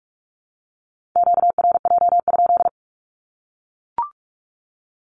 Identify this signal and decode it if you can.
Morse code, heard as a keyed tone, about 44 wpm, 700 Hz. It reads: QRJ?
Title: Q-Codes Repeated 3x 45wpm